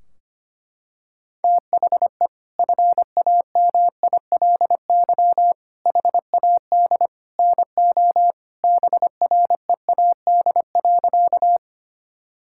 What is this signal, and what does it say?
Morse code, heard as a keyed tone, about 25 wpm, 700 Hz. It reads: THE FAMILY HAD NO BREAD.
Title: Sentences from Top 1600 Words 25wpm